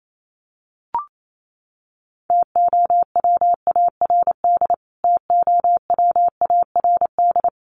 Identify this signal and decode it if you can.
Morse code, heard as a keyed tone, about 28 wpm, 700 Hz. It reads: TOWARD TOWARD
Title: Top 500 Words Repeated 28wpm